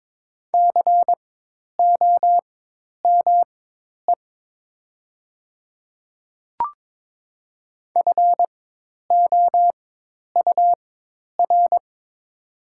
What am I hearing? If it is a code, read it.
Morse code, heard as a keyed tone, about 22 wpm, spaced as for 10 wpm, 700 Hz. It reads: COME FOUR
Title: 4x Character Spacing - Top 300 Words - 22wpm